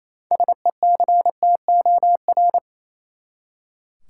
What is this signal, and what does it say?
Morse code, heard as a keyed tone, about 28 wpm, 700 Hz. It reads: SECTOR
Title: Six Letter Words - 28wpm